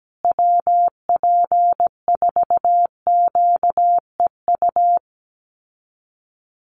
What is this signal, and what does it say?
Morse code, heard as a keyed tone, about 17 wpm, 700 Hz. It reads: WP4QEU
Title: USA Call Signs Repeated 3x 17wpm